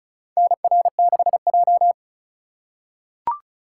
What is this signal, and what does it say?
Morse code, heard as a keyed tone, about 34 wpm, 700 Hz. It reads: NR6J
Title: USA Call Signs - 4 Characters - 4x Character Spacing - 35wpm